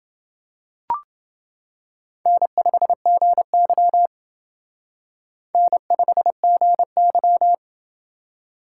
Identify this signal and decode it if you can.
Morse code, heard as a keyed tone, about 28 wpm, 700 Hz. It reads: N5GY N5GY